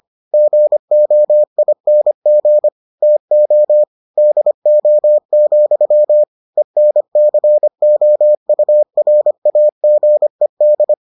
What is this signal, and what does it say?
Morse code, heard as a keyed tone, about 25 wpm, 600 Hz. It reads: GOING TO DO, ENCOURAGED